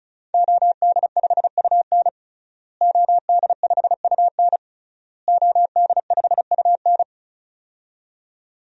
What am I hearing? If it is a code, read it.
Morse code, heard as a keyed tone, about 34 wpm, 700 Hz. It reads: OD5UN OD5UN OD5UN
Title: DXCC-100 Call Signs - 5 Characters - Repeated 3x - 35wpm